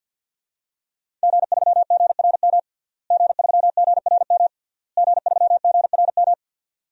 Morse code, heard as a keyed tone, about 52 wpm, 700 Hz. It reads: G3GRM G3GRM G3GRM
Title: DXCC-100 Call Signs - 5 Characters - Repeated 3x - 50wpm